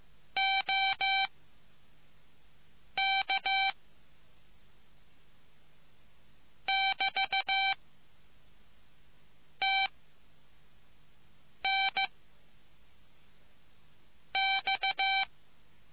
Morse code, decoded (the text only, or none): OK <BT> TN X